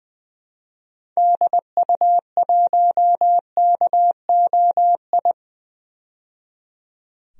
Morse code, heard as a keyed tone, about 20 wpm, 700 Hz. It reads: DU1KOI
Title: DXCC-100 Call Signs - 6 Characters - 20wpm